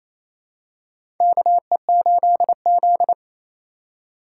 Morse code, heard as a keyed tone, about 28 wpm, 700 Hz. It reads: KE8Z
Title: USA Call Signs - 4 Characters - 2x Character Spacing - 28wpm